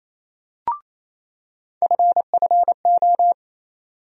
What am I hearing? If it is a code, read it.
Morse code, heard as a keyed tone, about 28 wpm, 700 Hz. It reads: FFO
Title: USA Call Sign Postfixes - 3 Letters 28wpm